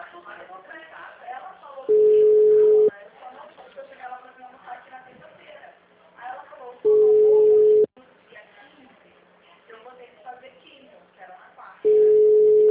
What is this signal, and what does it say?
Ringback tone (Europe, 425 Hz), with speech.
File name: out- -2016- - - .326